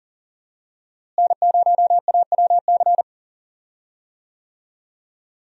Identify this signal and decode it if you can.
Morse code, heard as a keyed tone, about 40 wpm, 700 Hz. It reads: N0AWC
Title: USA Call Signs - 5 Characters - Repeated 2x - 40wpm